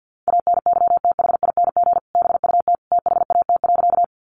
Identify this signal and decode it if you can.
Morse code, heard as a keyed tone, about 50 wpm, 700 Hz. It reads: ANYTHING BUT THAT.
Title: Sentences from Top 1000 Words 50wpm